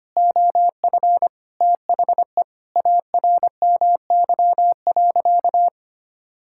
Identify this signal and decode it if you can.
Morse code, heard as a keyed tone, about 25 wpm, 700 Hz. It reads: OF THE ARMY.
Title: Sentences from Top 1600 Words 25wpm